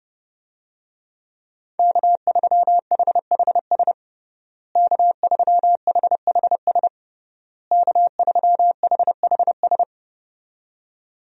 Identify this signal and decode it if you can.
Morse code, heard as a keyed tone, about 32 wpm, 700 Hz. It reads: K3HHS K3HHS K3HHS
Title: USA Call Signs - 5 Characters - Repeated 3x - 30wpm